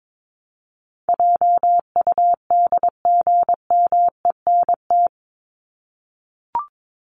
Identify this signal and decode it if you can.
Morse code, heard as a keyed tone, about 22 wpm, 700 Hz. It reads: JUDGMENT
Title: Top 1751-2000 Words 22wpm